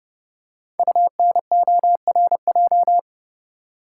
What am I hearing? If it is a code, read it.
Morse code, heard as a keyed tone, about 29 wpm, 700 Hz. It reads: UNORJ
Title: Code Groups - 5 Letters - 30wpm